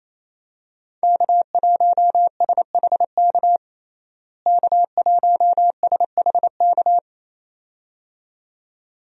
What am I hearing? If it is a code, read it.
Morse code, heard as a keyed tone, about 28 wpm, 700 Hz. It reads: K1SHK K1SHK
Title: USA Call Signs - 5 Characters - Repeated 2x - 28wpm